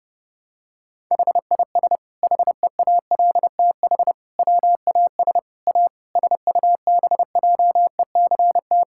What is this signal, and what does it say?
Morse code, heard as a keyed tone, about 30 wpm, 700 Hz. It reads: HIS HEALTH WAS A SUBJECT
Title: Sentences from Top 1600 Words 30wpm